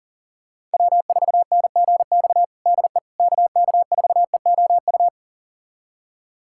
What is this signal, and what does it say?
Morse code, heard as a keyed tone, about 40 wpm, 700 Hz. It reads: W4NGX DE KK4EOU
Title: USA Callsign DE Callsign 40wpm